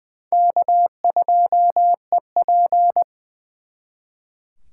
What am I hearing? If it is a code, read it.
Morse code, heard as a keyed tone, about 23 wpm, 700 Hz. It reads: K2EP
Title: Speed Racing - USA 1x2 Call Signs 20wpm